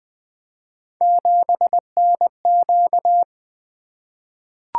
Morse code, heard as a keyed tone, about 20 wpm, 700 Hz. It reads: 7NQ
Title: Three Letter-Number Combinations - 20wpm